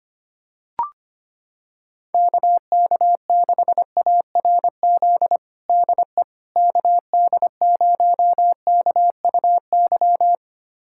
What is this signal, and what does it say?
Morse code, heard as a keyed tone, about 25 wpm, 700 Hz. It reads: KK6ARZ DE KD0KUY
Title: USA Callsign DE Callsign 25wpm